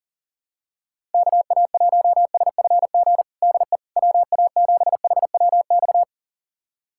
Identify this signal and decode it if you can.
Morse code, heard as a keyed tone, about 37 wpm, 700 Hz. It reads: KA1SFG DE WA7HWX
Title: USA Callsign DE Callsign 40wpm